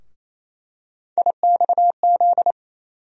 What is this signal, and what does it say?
Morse code, heard as a keyed tone, about 28 wpm, 700 Hz. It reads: IXZ
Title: USA Call Sign Postfixes - 3 Letters 28wpm